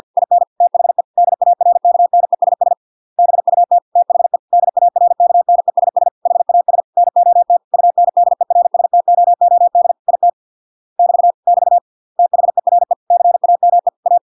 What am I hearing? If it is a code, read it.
Morse code, heard as a keyed tone, about 50 wpm, 700 Hz. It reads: IN THE DARKNESS BUT THE DARKNESS HAS NOT UNDERSTOOD IT <BT> <BT> THERE CAME A